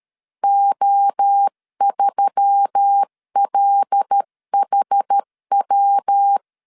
Morse code, heard as a keyed tone, about 13 wpm, 800 Hz. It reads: O3LHW
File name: CW / DET / 1 6 1500C S C 12wpm 3533